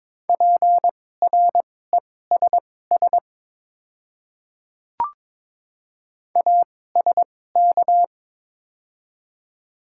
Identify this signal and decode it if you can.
Morse code, heard as a keyed tone, about 22 wpm, spaced as for 16 wpm, 690 Hz. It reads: PRESS ASK
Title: 2x Character Spacing - Top 300 Words - 22wpm